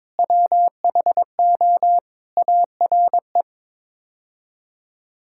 Morse code, heard as a keyed tone, about 22 wpm, 700 Hz. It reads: WHO ARE
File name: practice / 22wpm / Sets of 6 Words - 22wpm